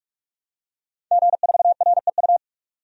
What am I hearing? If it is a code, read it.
Morse code, heard as a keyed tone, about 46 wpm, 700 Hz. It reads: G4REU
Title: DXCC-100 Call Signs - 5 Characters - 45wpm